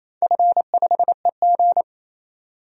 Morse code, heard as a keyed tone, about 27 wpm, 700 Hz. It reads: F5EG
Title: DXCC-100 Call Signs - 4 Characters - 28wpm